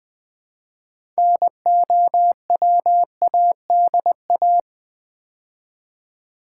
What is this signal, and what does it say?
Morse code, heard as a keyed tone, about 20 wpm, 700 Hz. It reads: NOWADA
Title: Build A Word - 2 Letters At A Time - 8-Letter Words - 20wpm